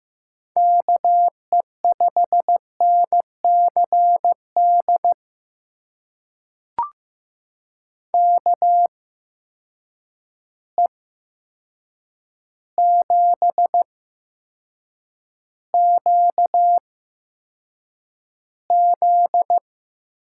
Morse code, heard as a keyed tone, about 15 wpm, spaced as for 7 wpm, 700 Hz. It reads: KE5NCD KE7QZ
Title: USA Call Signs - 6 Characters - 8x Character Spacing - 15wpm